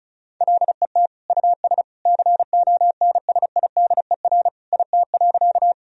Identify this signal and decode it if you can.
Morse code, heard as a keyed tone, about 34 wpm, 700 Hz. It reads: LET US CONSIDER IT.
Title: Sentences from Top 1000 Words - Review Part 4 - 35wpm